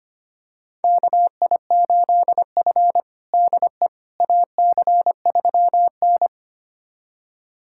Morse code, heard as a keyed tone, about 25 wpm, 700 Hz. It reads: KI8F DE AC3N
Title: USA Callsign DE Callsign - 4 Characters - 25wpm